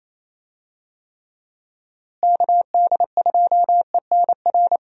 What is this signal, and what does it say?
Morse code, heard as a keyed tone, about 28 wpm, 700 Hz. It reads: KD2ENR
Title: USA Call Signs - 6 Characters - Repeated 2x - 28wpm